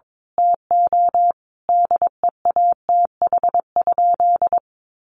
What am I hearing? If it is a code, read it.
Morse code, heard as a keyed tone, about 22 wpm, 700 Hz. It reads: TO DEATH?
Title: Sentences from Top 1100 Words 22wpm